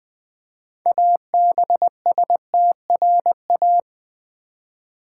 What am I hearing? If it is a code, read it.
Morse code, heard as a keyed tone, about 20 wpm, 700 Hz. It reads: ABSTRA
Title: Build A Word - 3 Letters At A Time - 8-Letter Words - 20wpm